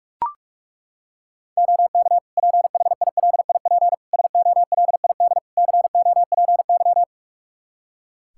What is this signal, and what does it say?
Morse code, heard as a keyed tone, about 41 wpm, 700 Hz. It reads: OK PHILIP SOLID COPY
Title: K1USN SST QSO Practice 45wpm